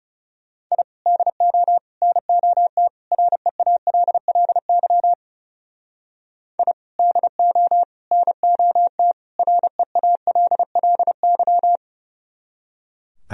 Morse code, heard as a keyed tone, about 35 wpm, 700 Hz. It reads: I DO NOT REALLY I DO NOT REALLY
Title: Speed-Racing - Sets of 4 Words 30wpm